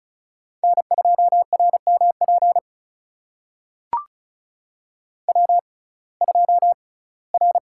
Morse code, heard as a keyed tone, about 36 wpm, spaced as for 21 wpm, 700 Hz. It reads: N2RMP W2R
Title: USA Call Signs - 5 Characters - 6x Character Spacing - 35wpm